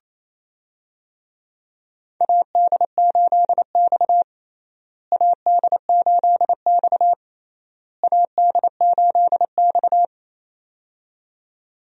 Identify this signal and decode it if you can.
Morse code, heard as a keyed tone, about 28 wpm, 700 Hz. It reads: AD8X AD8X AD8X
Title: USA Call Signs - 4 Characters - Repeated 3x - 28wpm